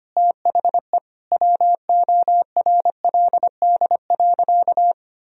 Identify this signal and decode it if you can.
Morse code, heard as a keyed tone, about 25 wpm, 700 Hz. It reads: THE WORLD.
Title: Sentences from Top 1000 Words - Review Part 4 - 25wpm